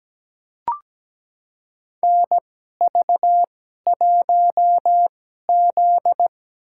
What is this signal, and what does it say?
Morse code, heard as a keyed tone, about 17 wpm, spaced as for 12 wpm, 700 Hz. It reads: NV1Z
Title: USA Call Signs - 4 Characters - 2x Character Spacing - 17wpm